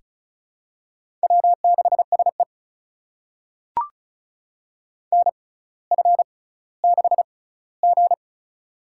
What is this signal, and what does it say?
Morse code, heard as a keyed tone, about 36 wpm, spaced as for 18 wpm, 700 Hz. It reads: W6SE NF6G
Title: USA Call Signs - 4 Characters - 6x Character Spacing - 35wpm